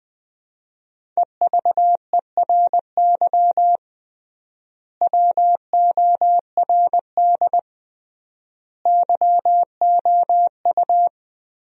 Morse code, heard as a keyed tone, about 20 wpm, 700 Hz. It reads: EVERY WORD YOU
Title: 3x Word-Spacing - Sentences from Top 400 Words - 20wpm